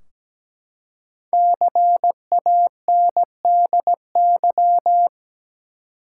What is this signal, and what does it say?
Morse code, heard as a keyed tone, about 17 wpm, 700 Hz. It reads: CANDY
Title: Five Letter Words - 17wpm